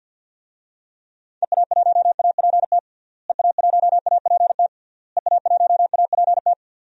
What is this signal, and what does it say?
Morse code, heard as a keyed tone, about 51 wpm, 700 Hz. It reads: EA1APT EA1APT EA1APT
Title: DXCC-100 Call Signs - 6 Characters - Repeated 3x - 50wpm